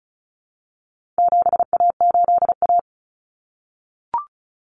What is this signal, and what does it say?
Morse code, heard as a keyed tone, about 34 wpm, 700 Hz. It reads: 7A8A
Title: Common Cut Numbers - 4-Digit 35wpm